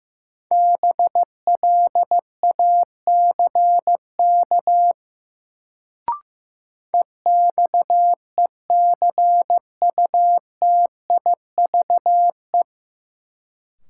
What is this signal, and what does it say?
Morse code, heard as a keyed tone, about 15 wpm, 700 Hz. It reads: BLACK EXECUTIVE
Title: Sets of 5 Words Encore - New Word Drill - 15wpm